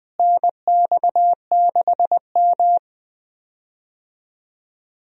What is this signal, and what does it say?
Morse code, heard as a keyed tone, about 20 wpm, 700 Hz. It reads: NX6M
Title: USA Call Signs - 4 Characters - Repeated 2x - 20wpm